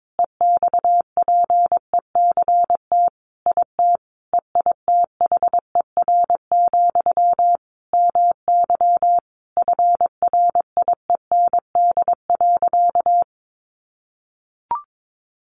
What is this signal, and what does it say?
Morse code, heard as a keyed tone, about 22 wpm, 700 Hz. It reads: EXPECT IT EITHER, MY FRIEND.
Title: Sentences from Top 1100 Words 22wpm